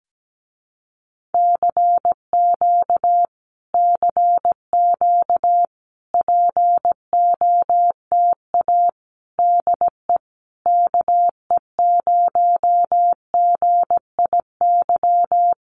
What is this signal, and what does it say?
Morse code, heard as a keyed tone, about 17 wpm, 700 Hz. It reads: CQ CQ POTA DE KE0GIY
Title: Common QSO Phrases Encore 17wpm